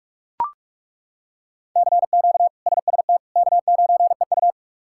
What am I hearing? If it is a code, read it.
Morse code, heard as a keyed tone, about 41 wpm, 700 Hz. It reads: CQ SST K9EU
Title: K1USN SST QSO Practice - 45wpm